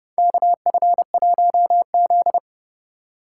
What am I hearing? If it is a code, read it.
Morse code, heard as a keyed tone, about 30 wpm, 700 Hz. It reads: KF1Z
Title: USA Call Signs - 4 Characters - 3x Character Spacing - 30wpm